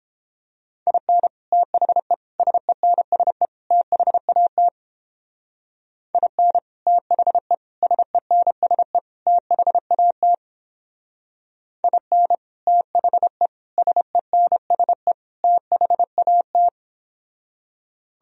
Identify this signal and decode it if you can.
Morse code, heard as a keyed tone, about 31 wpm, 700 Hz. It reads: IN THE SENSE THAT IN THE SENSE THAT IN THE SENSE THAT